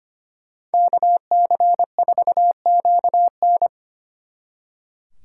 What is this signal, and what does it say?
Morse code, heard as a keyed tone, about 30 wpm, 700 Hz. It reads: KC4QN
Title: Speed Racing - USA Call Signs 25wpm